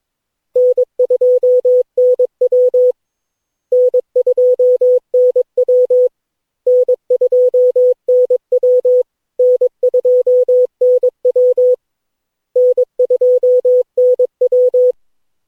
Morse code, heard as a keyed tone, about 22 wpm, 500 Hz. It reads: N2NW N2NW N2NW N2NW N2NW
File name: cw / Session9 cw training